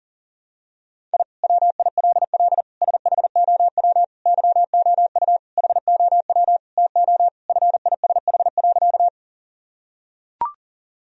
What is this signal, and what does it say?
Morse code, heard as a keyed tone, about 40 wpm, 700 Hz. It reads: I WILL SHOW YOU HOW TO FISH.
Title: Sentences from Top 1000 Words - Review Part 1 - 40wpm